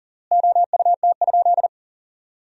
Morse code, heard as a keyed tone, about 40 wpm, 700 Hz. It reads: OUT?
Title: 3x Word-Spacing - Sentences from Top 300 Words - 40wpm